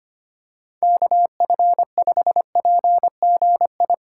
Morse code, heard as a keyed tone, about 25 wpm, 700 Hz. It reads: KF5PGI
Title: USA Call Signs - 3x Character Spacing - 25wpm